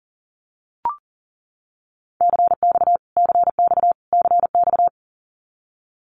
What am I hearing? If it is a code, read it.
Morse code, heard as a keyed tone, about 40 wpm, 700 Hz. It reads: CX CX CX
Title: Abbreviations Repeated 3x - 40wpm